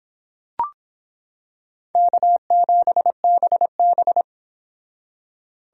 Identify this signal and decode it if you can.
Morse code, heard as a keyed tone, about 26 wpm, 700 Hz. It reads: K7BB